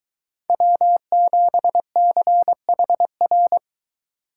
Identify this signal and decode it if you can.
Morse code, heard as a keyed tone, about 22 wpm, 700 Hz. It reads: W7CHR